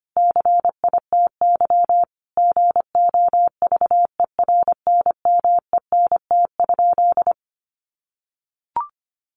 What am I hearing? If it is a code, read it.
Morse code, heard as a keyed tone, about 25 wpm, 700 Hz. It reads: CITY GOVERNMENT?
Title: Sentences from Top 1400 Words 25wpm